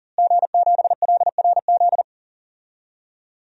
Morse code, heard as a keyed tone, about 40 wpm, 700 Hz. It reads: G7LRZ
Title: DXCC-100 Call Signs - 5 Characters - Repeated 2x - 40wpm